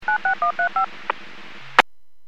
DTMF, heard as telephone touch tones, tones of about 110 ms. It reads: 96135
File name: WPS / 06977 21-07 WPS 451.25 WPS